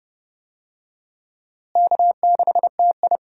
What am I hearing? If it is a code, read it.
Morse code, heard as a keyed tone, about 30 wpm, 700 Hz. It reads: K6TI
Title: USA Call Signs - 4 Characters - Repeated 2x - 30wpm